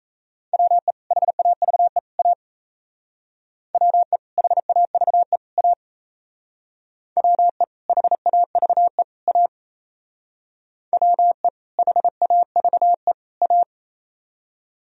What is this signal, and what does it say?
Morse code, heard as a keyed tone, about 37 wpm, 700 Hz. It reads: WE HAVE A WE HAVE A WE HAVE A WE HAVE A